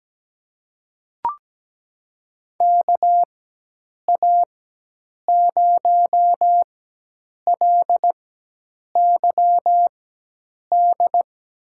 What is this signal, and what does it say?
Morse code, heard as a keyed tone, about 17 wpm, spaced as for 8 wpm, 700 Hz. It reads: KA0LYD
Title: USA Call Signs - 6 Characters - 4x Character Spacing - 17wpm